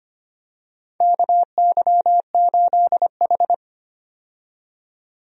Morse code, heard as a keyed tone, about 25 wpm, 700 Hz. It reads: KY8H
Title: USA Call Signs - 4 Characters - 25wpm